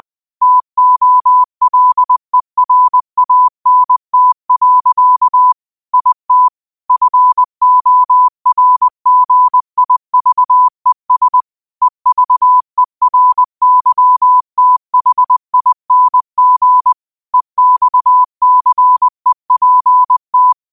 Morse code, heard as a keyed tone, about 20 wpm, 1000 Hz. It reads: TOLERANT. IT FORGIVES EVERYTHING EXCEPT
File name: qotdfiles / 20wpm